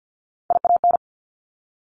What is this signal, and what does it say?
Morse code, heard as a keyed tone, about 50 wpm, 700 Hz. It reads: IAN
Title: Top Three Letter Combinations in Words 50wpm